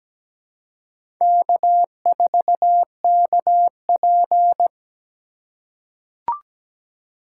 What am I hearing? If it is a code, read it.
Morse code, heard as a keyed tone, about 17 wpm, 700 Hz. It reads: K4KP
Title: USA Call Signs - 4 Characters - 3x Character Spacing - 17wpm